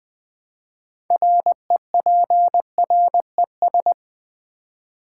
Morse code, heard as a keyed tone, about 20 wpm, 700 Hz. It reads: REPRES
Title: Build A Word - 3 Letters At A Time - 9-Letter Words - 20wpm